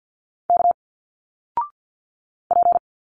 Morse code, heard as a keyed tone, about 50 wpm, 700 Hz. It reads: K P
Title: Single Letters 50wpm